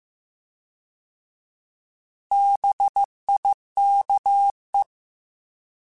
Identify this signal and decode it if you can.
Morse code, heard as a keyed tone, about 15 wpm, 800 Hz. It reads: BIKE
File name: Alphabetical / Alphabetical mp3s / Bs @13